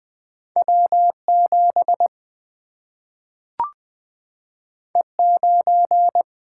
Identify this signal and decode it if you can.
Morse code, heard as a keyed tone, about 20 wpm, 700 Hz. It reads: W 7 E 9
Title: Single-Letter Single-Number 20wpm